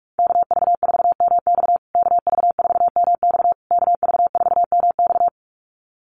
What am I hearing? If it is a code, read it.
Morse code, heard as a keyed tone, about 44 wpm, 700 Hz. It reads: KV4MX KV4MX KV4MX
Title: USA Call Signs Repeated 3x 45wpm